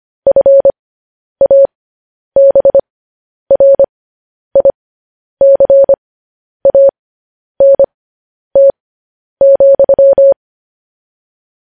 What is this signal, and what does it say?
Morse code, heard as a keyed tone, about 25 wpm, spaced as for 10 wpm, 550 Hz. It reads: FABRICANT,
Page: news headlines in morse code at 10 wpm